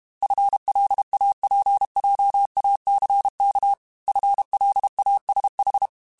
Morse code, heard as a keyed tone, about 32 wpm, 800 Hz. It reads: FLAPJACK FLASH
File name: FL Runs / FL Runs mp3s / FL Runs @30